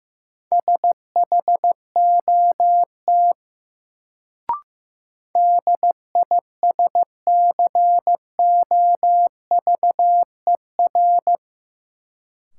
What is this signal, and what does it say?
Morse code, heard as a keyed tone, about 15 wpm, 700 Hz. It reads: SHOT DISCOVER
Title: Sentences from Top 1100 Words - New Word Drill - 15wpm